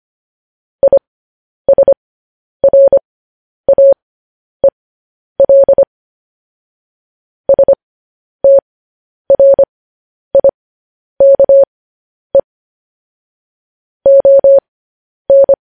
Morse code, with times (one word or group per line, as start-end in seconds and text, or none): 0.8-5.8 ISRAEL
7.5-12.4 STRIKE
14.1-15.5 ON